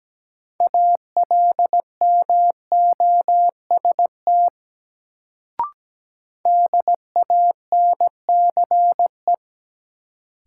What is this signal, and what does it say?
Morse code, heard as a keyed tone, about 17 wpm, 700 Hz. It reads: ALMOST DANCE
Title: Sentences from Top 700 Words - New Word Drill - 17wpm